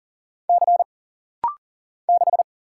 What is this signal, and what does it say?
Morse code, heard as a keyed tone, about 40 wpm, 700 Hz. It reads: C 6
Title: Single Letter-Number Alt - Rapid-Fire 40wpm